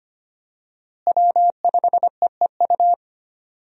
Morse code, heard as a keyed tone, about 25 wpm, 700 Hz. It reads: W5EEU
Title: USA Call Signs - 5 Characters - Repeated 2x - 25wpm